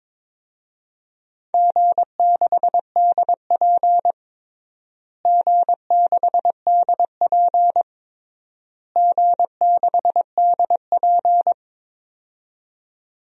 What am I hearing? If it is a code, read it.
Morse code, heard as a keyed tone, about 22 wpm, 700 Hz. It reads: G6DP G6DP G6DP
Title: DXCC-100 Call Signs - 4 Characters - Repeated 3x - 22wpm